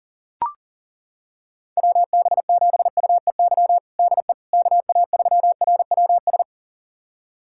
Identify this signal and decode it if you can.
Morse code, heard as a keyed tone, about 37 wpm, 700 Hz. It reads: WB7UEY DE KA3RWS
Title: USA Callsign DE Callsign - 6 Characters - 40wpm